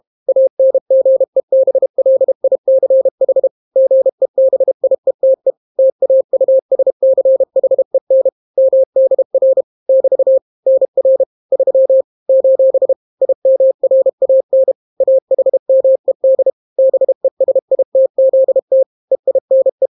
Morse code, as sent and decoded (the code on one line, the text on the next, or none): .- -. --. -... .-.. .. -.-. .... / --. -... .. - . / - .- ..- ... -.-. .... -. / -- -.. .-. / -...- / -. .-. / ...-- / ---... / .. -- .-. .- -. / .- .... -- . -.. / -... ... .. - --.. - / . .. -. .
ANGEBLICH GEBIETE TAUSCHEN MDR <BT> NR 3 : IMRAN AHMED BESITZT EINE